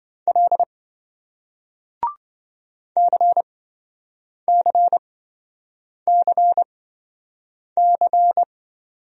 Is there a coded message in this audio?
Yes: Morse code from 0.3 to 8.4 s, heard as a keyed tone, about 28 wpm, spaced as for 7 wpm, 700 Hz. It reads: L CCCC